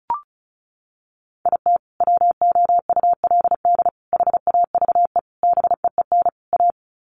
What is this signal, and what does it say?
Morse code, heard as a keyed tone, about 34 wpm, 700 Hz. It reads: IT WOULD HAVE BEEN A
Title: Sets of 5 Words Encore 35wpm